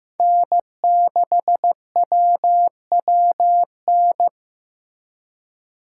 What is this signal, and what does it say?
Morse code, heard as a keyed tone, about 15 wpm, 700 Hz. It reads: N6WWN
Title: USA Call Signs - 5 Characters - 2x Character Spacing - 15wpm